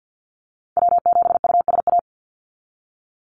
Morse code, heard as a keyed tone, about 50 wpm, 700 Hz. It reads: W7USA
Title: NA SOTA Activator Callsigns 50wpm